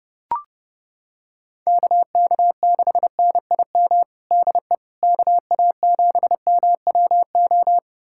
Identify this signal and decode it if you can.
Morse code, heard as a keyed tone, about 30 wpm, 700 Hz. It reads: KK6NIM DE KA7MWO
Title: USA Callsign DE Callsign - 6 Characters - 30wpm